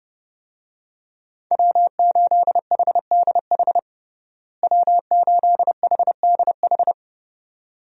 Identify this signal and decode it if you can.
Morse code, heard as a keyed tone, about 30 wpm, 700 Hz. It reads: W8HDH W8HDH
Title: USA Call Signs - 5 Characters - Repeated 2x - 30wpm